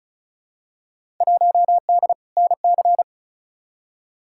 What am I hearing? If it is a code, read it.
Morse code, heard as a keyed tone, about 34 wpm, 700 Hz. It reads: 1D NC
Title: Field Day Exchanges Encore - 35wpm